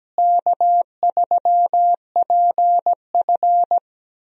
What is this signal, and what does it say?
Morse code, heard as a keyed tone, about 17 wpm, 700 Hz. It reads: K3PF
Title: USA Call Signs - 4 Characters - Repeated 2x - 17wpm